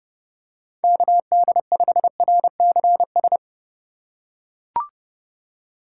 Morse code, heard as a keyed tone, about 31 wpm, 700 Hz. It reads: KD5RCS